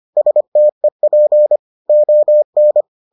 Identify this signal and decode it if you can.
Morse code, heard as a keyed tone, about 25 wpm, 600 Hz. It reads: STEP ON